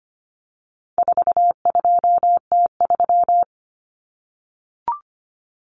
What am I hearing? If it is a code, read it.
Morse code, heard as a keyed tone, about 25 wpm, 700 Hz. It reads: 42T3
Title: Common Cut Numbers - 4-Digit 25wpm